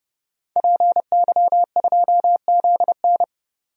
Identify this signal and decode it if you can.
Morse code, heard as a keyed tone, about 30 wpm, 700 Hz. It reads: PY2ZN
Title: DXCC-100 Call Signs - 5 Characters - Repeated 3x - 30wpm